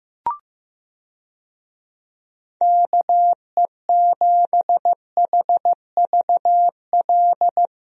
Morse code, heard as a keyed tone, about 15 wpm, 700 Hz. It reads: KE7HVL
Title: USA Call Signs - 6 Characters - Repeated 2x - 15wpm